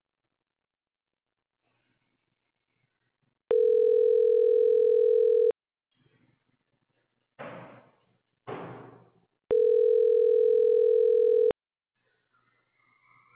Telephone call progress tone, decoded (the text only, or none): ringback tone